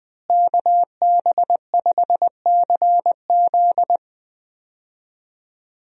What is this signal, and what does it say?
Morse code, heard as a keyed tone, about 20 wpm, 700 Hz. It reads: KB5CZ
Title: USA Call Signs - 5 Characters - Repeated 2x - 20wpm